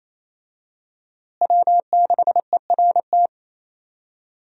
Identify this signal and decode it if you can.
Morse code, heard as a keyed tone, about 28 wpm, 700 Hz. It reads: W6ERT
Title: USA Call Signs - 5 Characters - Repeated 2x - 28wpm